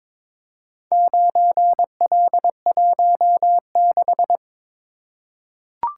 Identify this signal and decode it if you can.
Morse code, heard as a keyed tone, about 22 wpm, 700 Hz. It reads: 9L16
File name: practice / 22wpm / Four Letter-Number Combinations - 22wpm